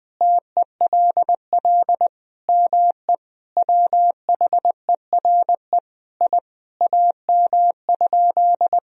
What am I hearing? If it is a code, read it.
Morse code, heard as a keyed tone, about 20 wpm, 700 Hz. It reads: TELL ME WHERE I AM?
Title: Sentences from Top 1000 Words - Review Part 2 - 20wpm